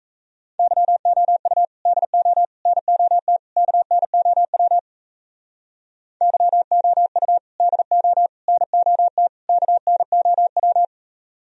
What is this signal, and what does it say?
Morse code, heard as a keyed tone, about 34 wpm, 700 Hz. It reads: YOU DO NOT KNOW YOU DO NOT KNOW
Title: Speed-Racing - Sets of 4 Words 28wpm